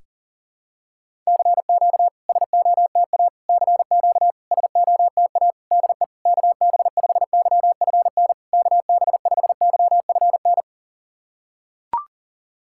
Morse code, heard as a keyed tone, about 40 wpm, 690 Hz. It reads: CQ SOTA CQ SOTA DE KB5YFN KB5YFN
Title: SOTA QSO Practice 40wpm